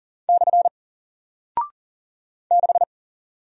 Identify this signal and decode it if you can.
Morse code, heard as a keyed tone, about 40 wpm, 700 Hz. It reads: / 6
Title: Single Character (Letter:Number:Punctuation:Prosign) Alt 40wpm